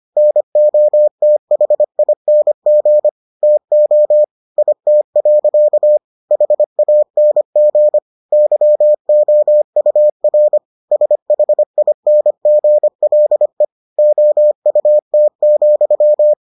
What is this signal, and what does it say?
Morse code, heard as a keyed tone, about 25 wpm, 600 Hz. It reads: NOTHING TO IT. HANG YOUR SHINGLE OUT,